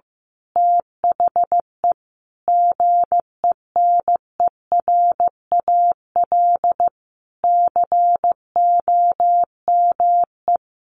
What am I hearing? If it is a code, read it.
Morse code, heard as a keyed tone, about 15 wpm, 700 Hz. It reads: THE GENERAL COME
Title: Sentences from Top 1400 Words 15wpm